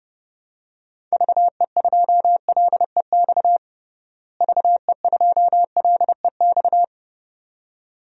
Morse code, heard as a keyed tone, about 30 wpm, 700 Hz. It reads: VE2LEX VE2LEX
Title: DXCC-100 Call Signs - 6 Characters - Repeated 2x - 30wpm